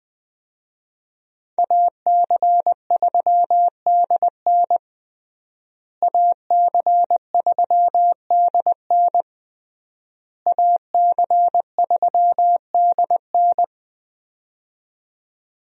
Morse code, heard as a keyed tone, about 20 wpm, 700 Hz. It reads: AC3DN AC3DN AC3DN
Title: USA Call Signs - 5 Characters - Repeated 3x - 20wpm